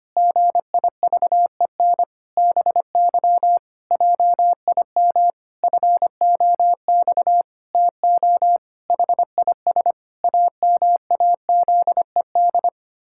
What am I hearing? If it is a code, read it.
Morse code, heard as a keyed tone, about 25 wpm, 700 Hz. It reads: GIVEN BY JIM FOX TO HIS AMAZED
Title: Pangram Challenge v2 - 25wpm